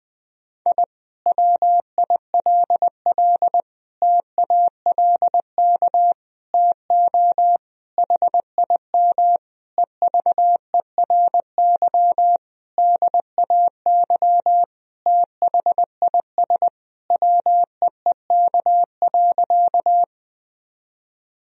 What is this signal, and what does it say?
Morse code, heard as a keyed tone, about 20 wpm, 700 Hz. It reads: I WILL TALK TO HIM EVERY DAY THIS WEEK.
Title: Sentences from Top 1000 Words - Review Part 2 - 20wpm